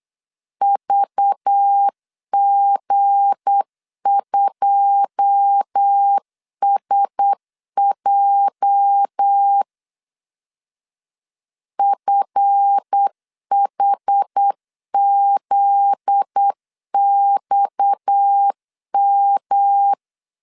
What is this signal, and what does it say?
Morse code, heard as a keyed tone, about 8 wpm, 800 Hz. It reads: VG2SJ FHZXM
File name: CW / DET / 1 9 1500C S C 8wpm 5653